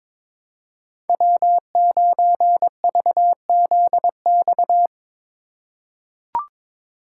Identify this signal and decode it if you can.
Morse code, heard as a keyed tone, about 22 wpm, 700 Hz. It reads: W9VZX
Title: USA Call Signs - 5 Characters - 3x Character Spacing - 22wpm